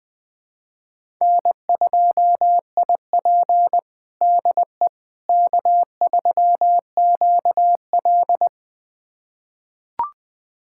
Morse code, heard as a keyed tone, about 20 wpm, 700 Hz. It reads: N2IP DE K3QL
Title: USA Callsign DE Callsign - 4 Characters - 20wpm